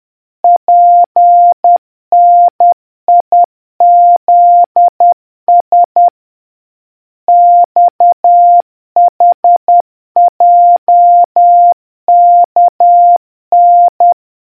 Morse code, heard as a keyed tone, about 10 wpm, 700 Hz. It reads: PNIZS XHJKN